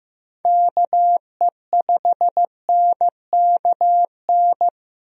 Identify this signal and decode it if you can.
Morse code, heard as a keyed tone, about 15 wpm, 700 Hz. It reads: KE5NKN
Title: USA Call Signs - 6 Characters - Repeated 3x - 15wpm